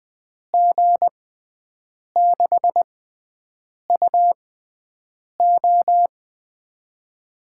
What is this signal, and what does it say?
Morse code, heard as a keyed tone, about 20 wpm, spaced as for 7 wpm, 700 Hz. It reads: G6UO
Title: USA Call Signs - 5 Characters - 6x Character Spacing - 20wpm